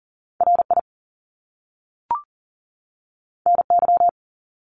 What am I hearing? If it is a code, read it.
Morse code, heard as a keyed tone, about 40 wpm, 700 Hz. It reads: RI NY
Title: US States 40wpm